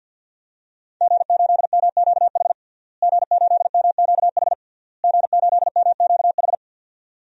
Morse code, heard as a keyed tone, about 52 wpm, 700 Hz. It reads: G8MQH G8MQH G8MQH
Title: DXCC-100 Call Signs - 5 Characters - Repeated 3x - 50wpm